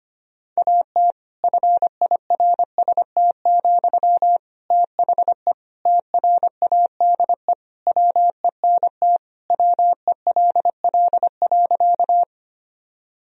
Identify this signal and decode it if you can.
Morse code, heard as a keyed tone, about 25 wpm, 700 Hz. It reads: AT FIRST, THE TRADE WENT WELL.
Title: Sentences from Top 1000 Words - Review Part 4 - 25wpm